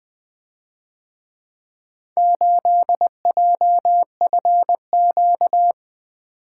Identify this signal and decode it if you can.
Morse code, heard as a keyed tone, about 21 wpm, 700 Hz. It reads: 8JFQ
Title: Speed-Racing - Four Letter-Number Combinations - 20wpm